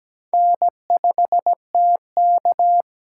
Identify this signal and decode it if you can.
Morse code, heard as a keyed tone, about 17 wpm, 700 Hz. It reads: N5TK
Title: USA Call Signs - 4 Characters - 4x Character Spacing - 17wpm